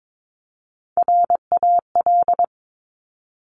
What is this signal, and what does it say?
Morse code, heard as a keyed tone, about 22 wpm, 700 Hz. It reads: RAL
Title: Top Three Letter Combinations in Words 22wpm